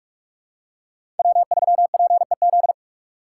Morse code, heard as a keyed tone, about 46 wpm, 700 Hz. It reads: W3PEZ
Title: USA Call Signs - 3x Character Spacing - 45wpm